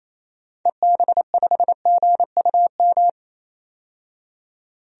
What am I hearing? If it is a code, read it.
Morse code, heard as a keyed tone, about 28 wpm, 700 Hz. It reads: EB5GUM
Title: DXCC-100 Call Signs - 6 Characters - 28wpm